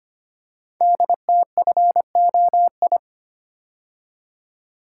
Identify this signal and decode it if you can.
Morse code, heard as a keyed tone, about 25 wpm, 700 Hz. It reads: DTFOI
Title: Code Groups - 5 Letters - 25wpm